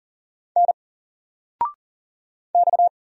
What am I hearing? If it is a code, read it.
Morse code, heard as a keyed tone, about 39 wpm, 700 Hz. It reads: N X
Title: Single Letter-Number 40wpm